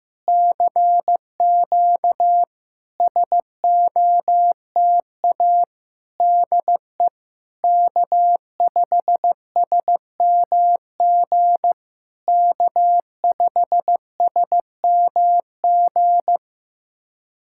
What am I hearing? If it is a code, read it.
Morse code, heard as a keyed tone, about 15 wpm, 700 Hz. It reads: CQ SOTA DE K5SMG K5SMG
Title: SOTA QSO Practice 15wpm